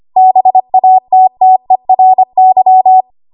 Morse code, heard as a keyed tone, about 25 wpm, 750 Hz. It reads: BATTERY
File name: Archive / 25 WPM / 25WPM